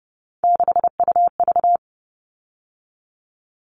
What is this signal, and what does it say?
Morse code, heard as a keyed tone, about 30 wpm, 700 Hz. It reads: 6UV
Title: All Cut Numbers - 3-Digit 30wpm